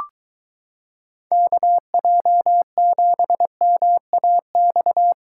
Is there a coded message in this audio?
Yes: Morse code from 1.3 to 5.1 s, heard as a keyed tone, about 23 wpm, 700 Hz. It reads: KJ7MAX